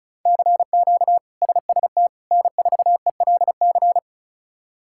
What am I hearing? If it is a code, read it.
Morse code, heard as a keyed tone, about 34 wpm, 700 Hz. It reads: CQ SST N4ELC
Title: K1USN SST QSO Practice - 35wpm